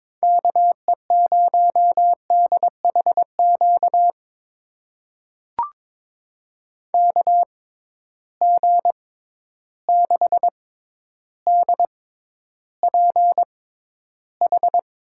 Morse code, heard as a keyed tone, about 22 wpm, spaced as for 11 wpm, 700 Hz. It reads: KE0DHQ KG6DPH
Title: USA Call Signs - 6 Characters - 6x Character Spacing - 22wpm